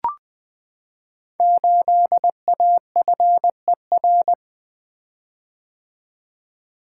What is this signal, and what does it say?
Morse code, heard as a keyed tone, about 20 wpm, 700 Hz. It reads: 8AFER
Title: Code Groups - 5 Letter:Numbers - 20wpm